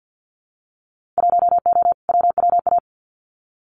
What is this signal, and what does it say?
Morse code, heard as a keyed tone, about 50 wpm, 700 Hz. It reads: 1O WWA
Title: Winter Field Day Exchanges 50wpm